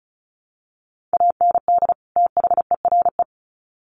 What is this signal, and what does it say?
Morse code, heard as a keyed tone, about 34 wpm, 700 Hz. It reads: AND THERE
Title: Sets of 2 Words 35wpm